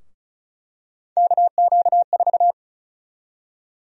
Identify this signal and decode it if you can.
Morse code, heard as a keyed tone, about 34 wpm, 700 Hz. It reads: KQ4
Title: USA Call Sign Prefixes - 35wpm